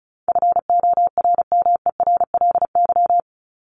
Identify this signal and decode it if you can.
Morse code, heard as a keyed tone, about 34 wpm, 700 Hz. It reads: FORMERLY
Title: LY Suffix - Top 100 Words with Base Words 35wpm